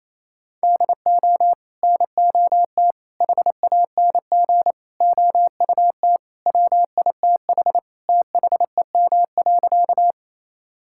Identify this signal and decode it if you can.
Morse code, heard as a keyed tone, about 27 wpm, 700 Hz. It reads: DO NOT HANG OUT WITH THEM.
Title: Sentences from Top 1600 Words 28wpm